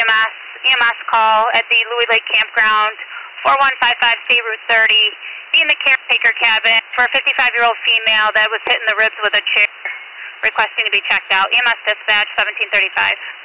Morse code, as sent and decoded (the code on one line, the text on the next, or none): none